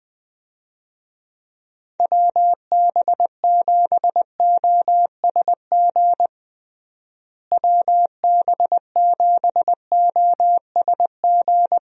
Morse code, heard as a keyed tone, about 20 wpm, 700 Hz. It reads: WB7OSG WB7OSG
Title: USA Call Signs - 6 Characters - Repeated 3x - 20wpm